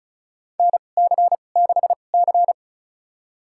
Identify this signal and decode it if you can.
Morse code, heard as a keyed tone, about 36 wpm, spaced as for 25 wpm, 700 Hz. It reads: NC6C
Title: USA Call Signs - 4 Characters - 2x Character Spacing - 35wpm